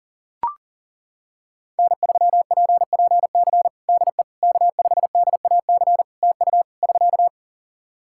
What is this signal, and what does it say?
Morse code, heard as a keyed tone, about 40 wpm, 700 Hz. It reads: N3PPC DE K5DAC TU <SK>
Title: K1USN SST QSO Practice 40wpm